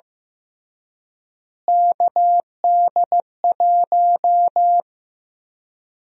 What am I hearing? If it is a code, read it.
Morse code, heard as a keyed tone, about 15 wpm, 700 Hz. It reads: KD1
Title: USA Call Sign Prefixes - Repeated 2x - 15wpm